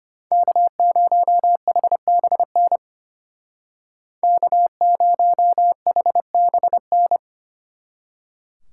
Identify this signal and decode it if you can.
Morse code, heard as a keyed tone, about 32 wpm, 700 Hz. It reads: K0HBN K0HBN